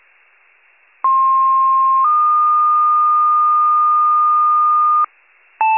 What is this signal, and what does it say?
Two-tone page, heard as a one-second tone, then a three-second tone, about 1060 then 1220 Hz, with noise. It is heard from 1.0 to 5.1 s.
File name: wcalert